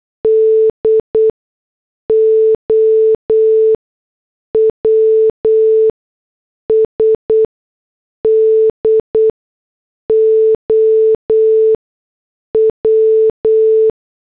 Morse code, heard as a keyed tone, about 8 wpm, spaced as for 6 wpm, 440 Hz. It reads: DOWS DOW
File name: morse / headlines